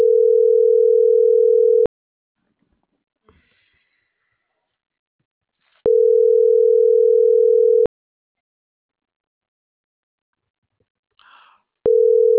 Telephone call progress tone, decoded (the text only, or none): ringback tone